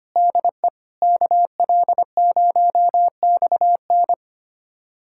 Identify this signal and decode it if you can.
Morse code, heard as a keyed tone, about 25 wpm, 700 Hz. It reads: DE KL0XN
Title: USA Callsign DE Callsign - 5 Characters - 25wpm